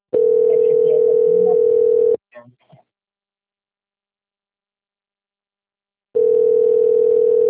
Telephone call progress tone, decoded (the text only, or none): ringback tone